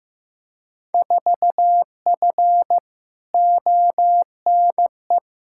Morse code, heard as a keyed tone, about 16 wpm, 700 Hz. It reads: 4F ONE
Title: Field Day Exchanges Encore - 15wpm